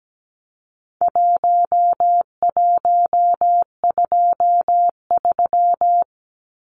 Morse code, heard as a keyed tone, about 17 wpm, 700 Hz. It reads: 1123
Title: Numbers - 4-Digit 17wpm